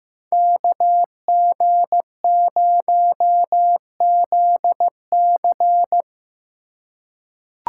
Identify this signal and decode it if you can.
Morse code, heard as a keyed tone, about 15 wpm, 700 Hz. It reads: KG0ZC
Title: USA Call Signs - 5 Characters - 2x Character Spacing - 15wpm